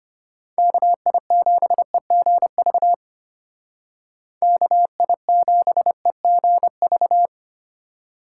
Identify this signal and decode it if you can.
Morse code, heard as a keyed tone, about 33 wpm, 700 Hz. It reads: KI7EGV KI7EGV